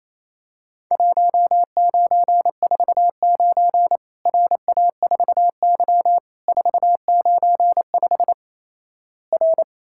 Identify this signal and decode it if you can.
Morse code, heard as a keyed tone, about 28 wpm, 700 Hz. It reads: 1949 RA4Y 495 R
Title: Worked All Europe DX Contest - QTC Exchange - 28wpm